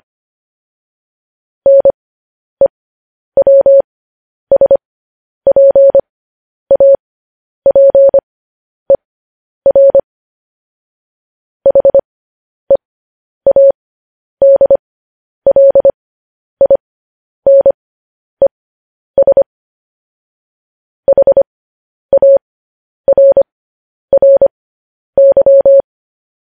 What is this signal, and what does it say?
Morse code, heard as a keyed tone, about 25 wpm, spaced as for 10 wpm, 550 Hz. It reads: NEWSPAPER HEADLINES HARRY
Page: News Headlines in Morse Code at 10 WPM